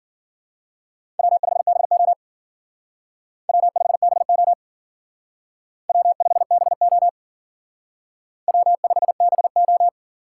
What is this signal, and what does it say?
Morse code, heard as a keyed tone, about 53 wpm, 700 Hz. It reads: W5BO W5BO W5BO W5BO